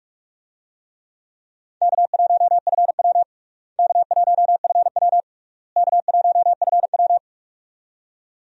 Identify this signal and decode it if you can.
Morse code, heard as a keyed tone, about 46 wpm, 700 Hz. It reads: K1FW K1FW K1FW
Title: USA Call Signs - 4 Characters - Repeated 3x - 45wpm